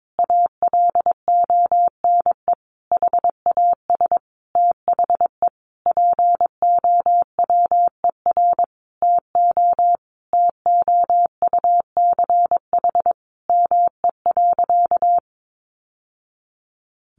Morse code, heard as a keyed tone, about 22 wpm, 700 Hz. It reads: ALONE HAS THE POWER TO TOUCH ME.
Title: Sentences from Top 1200 Words 22wpm